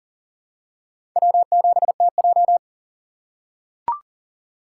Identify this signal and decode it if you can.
Morse code, heard as a keyed tone, about 37 wpm, 700 Hz. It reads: W7TJ
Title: USA Call Signs - 4 Characters - 4x Character Spacing - 40wpm